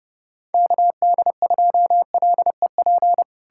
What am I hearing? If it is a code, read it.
Morse code, heard as a keyed tone, about 38 wpm, 700 Hz. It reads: KD2LEP
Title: Speed-Racing Common QSO Phrases Encore - 30wpm